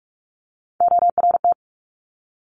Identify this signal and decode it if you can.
Morse code, heard as a keyed tone, about 46 wpm, 700 Hz. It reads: ORT
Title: Top Three Letter Combinations in Words 45wpm